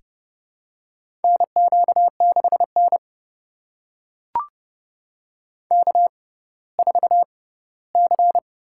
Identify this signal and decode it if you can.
Morse code, heard as a keyed tone, about 30 wpm, spaced as for 17 wpm, 700 Hz. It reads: NQ6N K4C
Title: USA Call Signs - 4 Characters - 6x Character Spacing - 30wpm